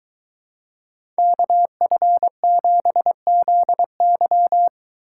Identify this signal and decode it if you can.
Morse code, heard as a keyed tone, about 24 wpm, 700 Hz. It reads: KF7ZY